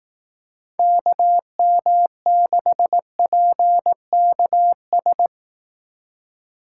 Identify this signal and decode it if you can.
Morse code, heard as a keyed tone, about 18 wpm, 700 Hz. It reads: KM6PKS
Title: Speed Racing - USA Call Signs 15wpm